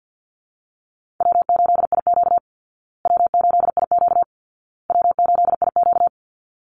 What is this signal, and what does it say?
Morse code, heard as a keyed tone, about 48 wpm, 700 Hz. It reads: W8IQ W8IQ W8IQ
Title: Easy Call Signs Repeated 3x - 50wpm